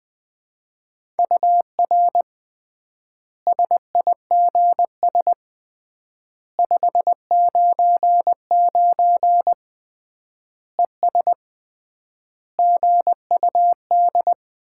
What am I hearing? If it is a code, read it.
Morse code, heard as a keyed tone, about 20 wpm, 700 Hz. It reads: UR SIGS 599 ES GUD
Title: Common QSO Phrases - 3x Word-Spacing - 20wpm